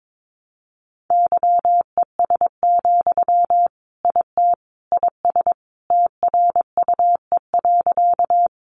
Morse code, heard as a keyed tone, about 22 wpm, 700 Hz. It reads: YES, IT IS TRUE.
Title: Sentences from Top 500 Words 22wpm